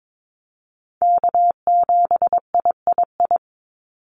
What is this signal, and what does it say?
Morse code, heard as a keyed tone, about 22 wpm, 700 Hz. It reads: K7III
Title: NA SOTA Chaser Callsigns 22wpm